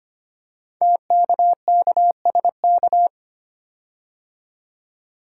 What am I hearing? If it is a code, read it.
Morse code, heard as a keyed tone, about 25 wpm, 700 Hz. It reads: TKKSK
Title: Code Groups - 5 Letters - 25wpm